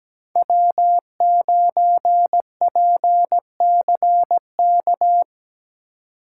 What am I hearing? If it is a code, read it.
Morse code, heard as a keyed tone, about 20 wpm, 700 Hz. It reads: W9PCK
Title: Speed Racing - USA 1x3 Call Signs 17wpm